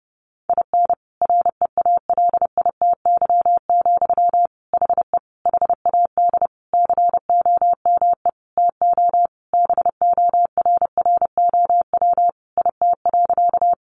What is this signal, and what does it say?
Morse code, heard as a keyed tone, about 28 wpm, 700 Hz. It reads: IN REALITY, HE HAD COME TO BORROW IT.
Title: Sentences from Top 1500 Words 30wpm